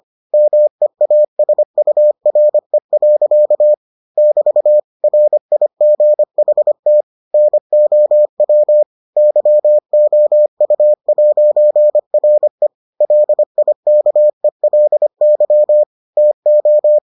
Morse code, heard as a keyed tone, about 25 wpm, 600 Hz. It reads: MEASURE. <BT> RIGHT NOW YOU'RE LIKELY TO